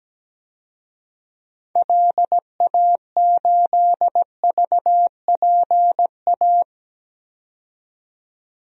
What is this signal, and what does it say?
Morse code, heard as a keyed tone, about 17 wpm, 700 Hz. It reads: LA8VPA
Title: DXCC-100 Call Signs - 6 Characters - 17wpm